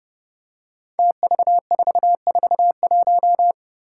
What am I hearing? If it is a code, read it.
Morse code, heard as a keyed tone, about 31 wpm, 700 Hz. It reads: TV441
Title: Five Challenging Letter-Number Combinations 30wpm